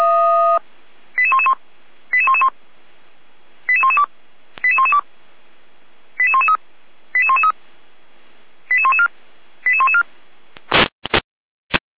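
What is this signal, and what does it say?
Five-tone selective call (ZVEI), heard as quick runs of five tones, tones of about 70 ms. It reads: 88111 88111 88112 88112 88113 88113 88114 88114
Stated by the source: Received via SDR; ZVEI five-tone selective call 88125.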